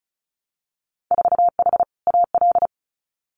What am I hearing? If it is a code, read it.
Morse code, heard as a keyed tone, about 36 wpm, 700 Hz. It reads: 4H AL
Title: Winter Field Day Exchanges Encore 35wpm